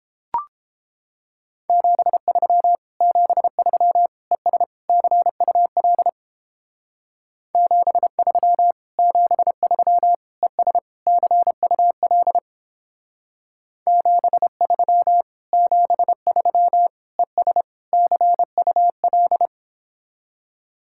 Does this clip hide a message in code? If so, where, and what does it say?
Yes: Morse code from 1.7 to 19.5 s, heard as a keyed tone, about 29 wpm, 700 Hz. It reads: 73 73 ES CUL 73 73 ES CUL 73 73 ES CUL